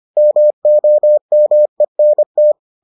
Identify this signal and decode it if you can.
Morse code, heard as a keyed tone, about 25 wpm, 600 Hz. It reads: MOMENT